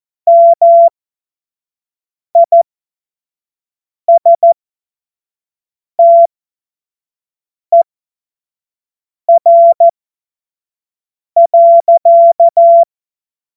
Morse code, heard as a keyed tone, about 14 wpm, spaced as for 5 wpm, 680 Hz. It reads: MISTER.
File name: Morsekurs / Witze / Witz 01. März 2023 25 F70